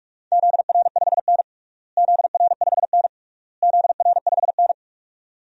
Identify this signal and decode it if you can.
Morse code, heard as a keyed tone, about 46 wpm, 700 Hz. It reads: ZR5N ZR5N ZR5N
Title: DXCC-100 Call Signs - 4 Characters - Repeated 3x - 45wpm